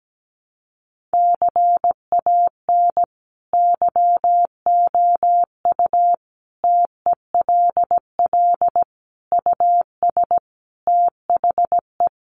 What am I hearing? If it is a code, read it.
Morse code, heard as a keyed tone, about 17 wpm, 700 Hz. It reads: CAN YOU TELL US THE
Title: Sentences from Top 1300 Words 17wpm